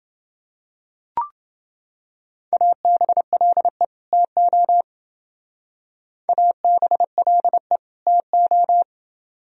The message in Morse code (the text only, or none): ABLE TO ABLE TO